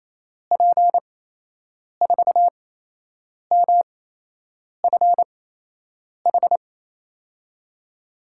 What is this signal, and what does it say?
Morse code, heard as a keyed tone, about 28 wpm, spaced as for 8 wpm, 700 Hz. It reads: P4MFH
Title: USA Call Signs - 8x Character Spacing - 28wpm